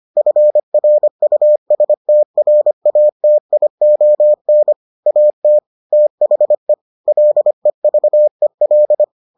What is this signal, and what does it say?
Morse code, heard as a keyed tone, about 25 wpm, 600 Hz. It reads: FRUSTRATION AT THE LEVEL